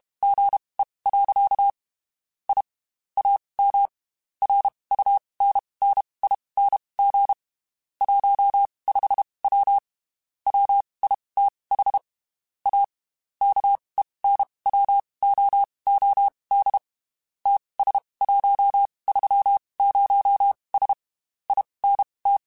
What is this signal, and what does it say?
Morse code, heard as a keyed tone, about 32 wpm, spaced as for 23 wpm, 800 Hz. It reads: GE. I AM RUNNING 15W WITH A KENWOOD TS130S INT